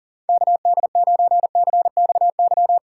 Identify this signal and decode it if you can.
Morse code, heard as a keyed tone, about 40 wpm, 700 Hz. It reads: KD9CXY
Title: USA Call Signs - 6 Characters - 4x Character Spacing - 40wpm